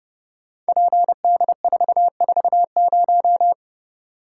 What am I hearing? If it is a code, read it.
Morse code, heard as a keyed tone, about 30 wpm, 700 Hz. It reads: PD440
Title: Code Groups - 5 Letter:Numbers - 30wpm